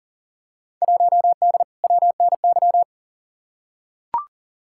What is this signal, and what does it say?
Morse code, heard as a keyed tone, about 39 wpm, 700 Hz. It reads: 1D WNY
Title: Field Day Exchanges - 40wpm